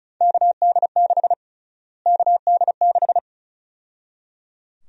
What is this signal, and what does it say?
Morse code, heard as a keyed tone, about 36 wpm, 700 Hz. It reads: KD6 KD6
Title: USA Call Sign Prefixes - Repeated 2x - 35wpm